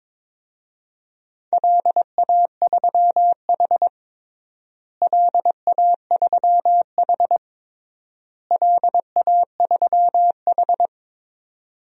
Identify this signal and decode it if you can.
Morse code, heard as a keyed tone, about 22 wpm, 700 Hz. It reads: LA3H LA3H LA3H
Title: DXCC-100 Call Signs - 4 Characters - Repeated 3x - 22wpm